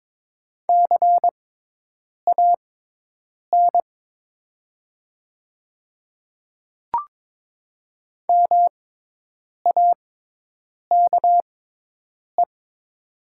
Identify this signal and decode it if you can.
Morse code, heard as a keyed tone, about 22 wpm, spaced as for 8 wpm, 700 Hz. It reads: CAN MAKE
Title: 6x Character Spacing - Top 100 Words - 22wpm